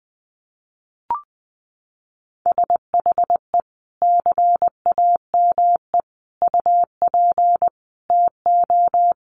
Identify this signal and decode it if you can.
Morse code, heard as a keyed tone, about 20 wpm, 700 Hz. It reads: SHE CAME UP TO
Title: Sentences from Top 200 Words 20wpm